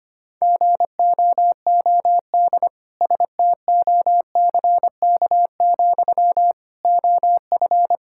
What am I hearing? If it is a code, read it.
Morse code, heard as a keyed tone, about 25 wpm, 700 Hz. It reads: GOOD STOCK, OF
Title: Sentences from Top 1000 Words - Review Part 5 - 25wpm